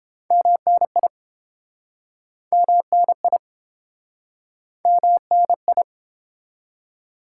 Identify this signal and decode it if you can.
Morse code, heard as a keyed tone, about 30 wpm, 700 Hz. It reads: MNI MNI MNI